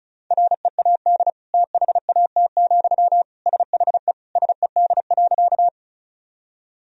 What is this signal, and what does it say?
Morse code, heard as a keyed tone, about 34 wpm, 700 Hz. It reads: READ THAT, SHE SED.
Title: Sentences from Top 200 Words and Abbreviations 35wpm